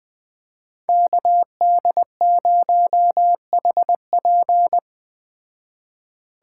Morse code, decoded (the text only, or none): KD0HP